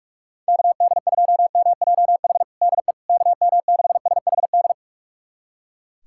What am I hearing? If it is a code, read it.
Morse code, heard as a keyed tone, about 42 wpm, 700 Hz. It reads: KD2MJH DE KM6SHD
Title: USA Callsign DE Callsign 45wpm